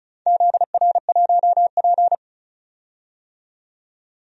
Morse code, heard as a keyed tone, about 36 wpm, 700 Hz. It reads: ZR1P
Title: DXCC-100 Call Signs - 4 Characters - 35wpm